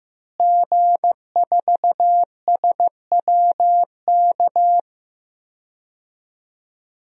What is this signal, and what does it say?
Morse code, heard as a keyed tone, about 15 wpm, 700 Hz. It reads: G4SWK
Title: DXCC-100 Call Signs - 5 Characters - Repeated 3x - 15wpm